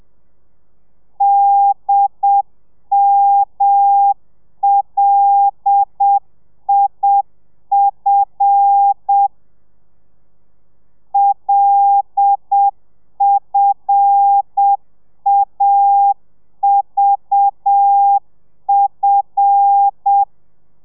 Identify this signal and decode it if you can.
Morse code, heard as a keyed tone, about 7 wpm, 800 Hz. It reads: DMLIF LFAVF
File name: telegrafia / i2tfj / Lez5 30